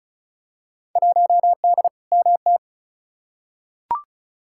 Morse code, heard as a keyed tone, about 34 wpm, 700 Hz. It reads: 1D MT
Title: Field Day Exchanges - 35wpm